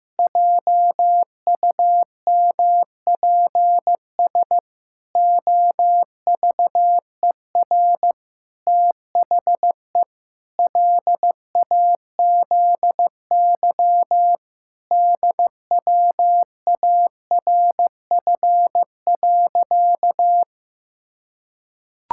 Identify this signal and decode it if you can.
Morse code, heard as a keyed tone, about 15 wpm, 700 Hz. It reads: JUMPS OVER THE LAZY DWARF.
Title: Pangram Challenge v2 - 15wpm